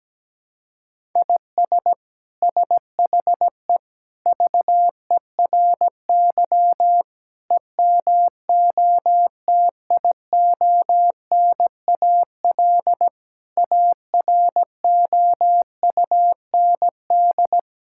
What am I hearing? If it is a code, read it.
Morse code, heard as a keyed tone, about 17 wpm, 700 Hz. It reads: IS SHE VERY EMOTIONAL AROUND
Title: Sentences from Top 1000 Words - Review Part 5 - 17wpm